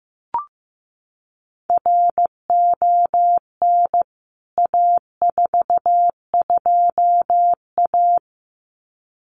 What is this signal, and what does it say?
Morse code, heard as a keyed tone, about 15 wpm, 700 Hz. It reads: RON A42A
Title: CWT Exchange 15wpm